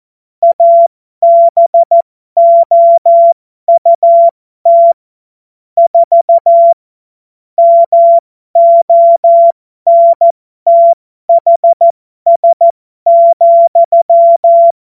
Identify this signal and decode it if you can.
Morse code, heard as a keyed tone, about 14 wpm, spaced as for 12 wpm, 680 Hz. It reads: ABOUT 4 MONTHS,